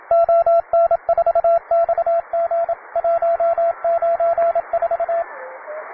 Morse code, heard as a keyed tone, about 27 wpm, 680 Hz, with radio noise. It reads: ON4XG 194
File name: wae / cw beginner / py2rh cw